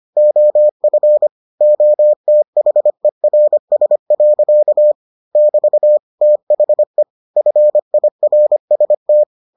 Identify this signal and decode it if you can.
Morse code, heard as a keyed tone, about 25 wpm, 600 Hz. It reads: OF OTHERS. <BT> THE FIRST